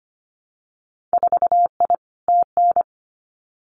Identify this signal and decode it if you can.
Morse code, heard as a keyed tone, about 25 wpm, 700 Hz. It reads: 4I TN
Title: Winter Field Day Exchanges 25wpm